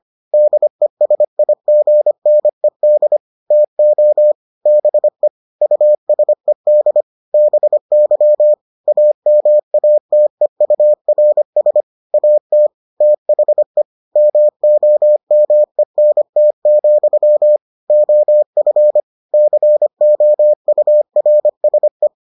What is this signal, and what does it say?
Morse code, heard as a keyed tone, about 25 wpm, 600 Hz. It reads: DESIGNED TO BE USED BY AMATEURS AT THE MOMENT, OF COURSE